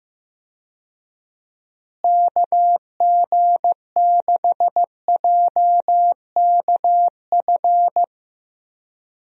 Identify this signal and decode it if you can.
Morse code, heard as a keyed tone, about 15 wpm, 700 Hz. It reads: KG6JKF
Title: USA Call Signs - 6 Characters - Repeated 2x - 15wpm